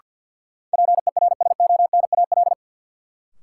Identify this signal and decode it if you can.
Morse code, heard as a keyed tone, about 47 wpm, 700 Hz. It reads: PERSONAL
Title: Sentences from Top 700 Words - New Word Drill - 50wpm